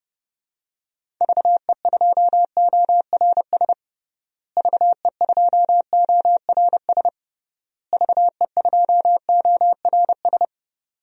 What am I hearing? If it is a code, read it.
Morse code, heard as a keyed tone, about 30 wpm, 700 Hz. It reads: VE2ORS VE2ORS VE2ORS
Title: DXCC-100 Call Signs - 6 Characters - Repeated 3x - 30wpm